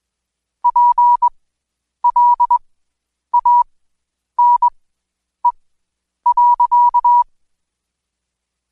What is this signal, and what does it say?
Morse code, heard as a keyed tone, about 21 wpm, spaced as for 9 wpm, 970 Hz. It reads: PLANE.
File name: Code Quick / AudioDownld2 / Seg115-Day20-DayDrill 20